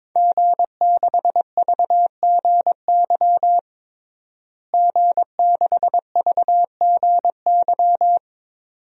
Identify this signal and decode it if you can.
Morse code, heard as a keyed tone, about 22 wpm, 700 Hz. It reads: G6VGY G6VGY
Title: DXCC-100 Call Signs - 5 Characters - Repeated 2x - 22wpm